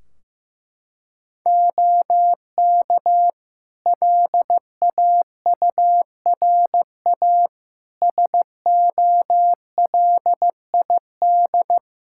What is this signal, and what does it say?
Morse code, heard as a keyed tone, about 15 wpm, 700 Hz. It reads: OK LAURA SOLID
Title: K1USN SST QSO Practice 15wpm